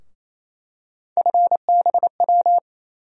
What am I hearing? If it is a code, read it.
Morse code, heard as a keyed tone, about 28 wpm, 700 Hz. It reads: FBW
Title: USA Call Sign Postfixes - 3 Letters 28wpm